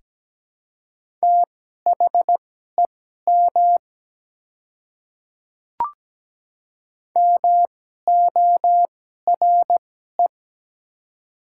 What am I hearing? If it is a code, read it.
Morse code, heard as a keyed tone, about 17 wpm, spaced as for 12 wpm, 700 Hz. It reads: THEM MORE
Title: 2x Character Spacing - Top 100 Words - 17wpm